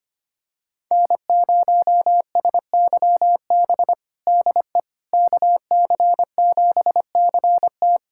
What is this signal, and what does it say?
Morse code, heard as a keyed tone, about 25 wpm, 700 Hz. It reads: N0SYB DE KC7CT
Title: USA Callsign DE Callsign - 5 Characters - 25wpm